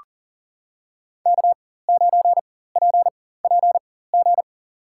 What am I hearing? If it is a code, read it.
Morse code, heard as a keyed tone, about 38 wpm, spaced as for 18 wpm, 700 Hz. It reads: K9PPG
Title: USA Call Signs - 5 Characters - 4x Character Spacing - 40wpm